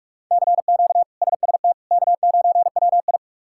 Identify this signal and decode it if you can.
Morse code, heard as a keyed tone, about 46 wpm, 700 Hz. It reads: CQ SST K9WI
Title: K1USN SST QSO Practice - 45wpm